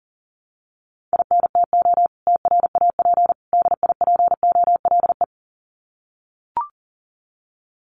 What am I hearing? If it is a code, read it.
Morse code, heard as a keyed tone, about 40 wpm, 700 Hz. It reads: INTO TRAP DIPOLE
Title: Common QSO Phrases 40wpm